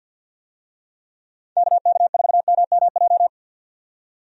Morse code, heard as a keyed tone, about 51 wpm, 700 Hz. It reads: KK4MMJ
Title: USA Call Signs - 6 Characters - 50wpm